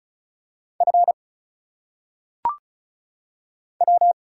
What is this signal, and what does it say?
Morse code, heard as a keyed tone, about 34 wpm, 700 Hz. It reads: F W
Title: Single Letters - Phonetic Pronunciation 35wpm